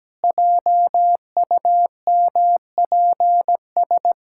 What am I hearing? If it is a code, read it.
Morse code, heard as a keyed tone, about 17 wpm, 700 Hz. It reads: JUMPS
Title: Pangram Challenge v2 - 17wpm